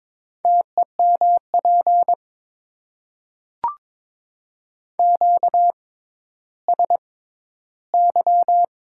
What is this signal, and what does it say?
Morse code, heard as a keyed tone, about 21 wpm, spaced as for 12 wpm, 700 Hz. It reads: TEMP QSY
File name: Top 90 QSO Elements - 6x Character Spacing - 22wpm